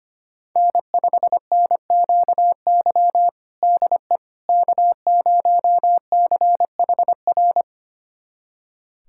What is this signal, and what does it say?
Morse code, heard as a keyed tone, about 25 wpm, 700 Hz. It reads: N5NQY DE K0CHR
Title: USA Callsign DE Callsign - 5 Characters - 25wpm